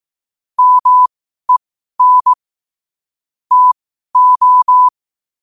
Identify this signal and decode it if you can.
Morse code, heard as a keyed tone, about 18 wpm, spaced as for 12 wpm, 1000 Hz. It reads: MEN TO
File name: qotdfiles / 12wpm